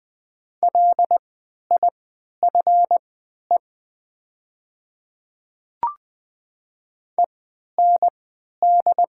Morse code, heard as a keyed tone, about 20 wpm, spaced as for 11 wpm, 700 Hz. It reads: LIFE END
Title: 3x Character Spacing - Top 300 Words - 20wpm